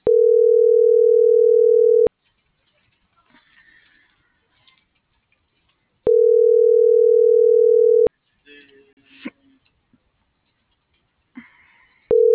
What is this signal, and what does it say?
Ringback tone (North America), 440 + 480 Hz.